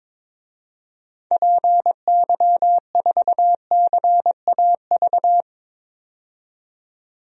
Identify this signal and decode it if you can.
Morse code, heard as a keyed tone, about 22 wpm, 700 Hz. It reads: PY4CAV
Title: DXCC-100 Call Signs - 6 Characters - 22wpm